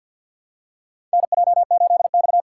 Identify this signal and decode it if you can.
Morse code, heard as a keyed tone, about 52 wpm, 700 Hz. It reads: NJ8X
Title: USA Call Signs - 4 Characters - 6x Character Spacing - 50wpm